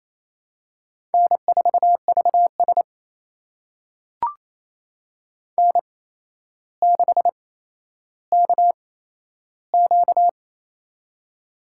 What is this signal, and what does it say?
Morse code, heard as a keyed tone, about 28 wpm, spaced as for 12 wpm, 700 Hz. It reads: N4VS N6KQ
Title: USA Call Signs - 4 Characters - 8x Character Spacing - 28wpm